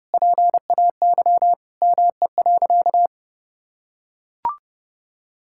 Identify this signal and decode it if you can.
Morse code, heard as a keyed tone, about 32 wpm, 700 Hz. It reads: PAY ME.
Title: Sentences from Top 1000 Words - Review Part 5 - 30wpm